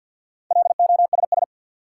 Morse code, heard as a keyed tone, about 47 wpm, 700 Hz. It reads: LOSS
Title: Four Letter Words - 50wpm